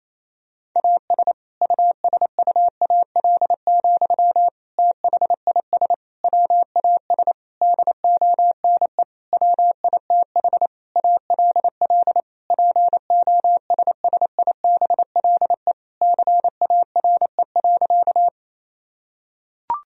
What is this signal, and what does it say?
Morse code, heard as a keyed tone, about 27 wpm, 700 Hz. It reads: AS USUAL, THIS WAS DONE WITH ALL POSSIBLE CARE.
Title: Sentences from Top 1000 Words - Review Part 2 - 28wpm